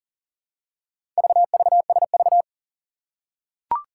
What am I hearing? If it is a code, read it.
Morse code, heard as a keyed tone, about 41 wpm, 700 Hz. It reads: VVSV
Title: Four Challenging Letter-Number Combinations 40wpm